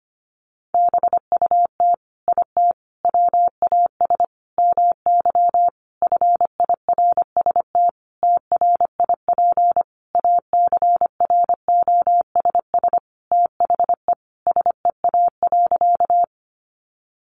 Morse code, heard as a keyed tone, about 25 wpm, 700 Hz. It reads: BUT IT WAS MY FIRST TRIP ACROSS THE SEA.
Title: Sentences from Top 1300 Words 25wpm